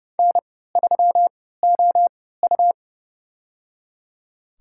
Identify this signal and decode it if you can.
Morse code, heard as a keyed tone, about 30 wpm, spaced as for 17 wpm, 700 Hz. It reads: N3OU
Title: USA Call Signs - 3x Character Spacing - 30wpm